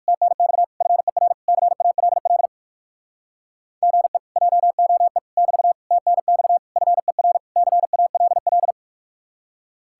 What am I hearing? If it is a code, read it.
Morse code, heard as a keyed tone, about 58 wpm, 700 Hz. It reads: TNX FER CALL GE JOE <BT> TNX FER CALL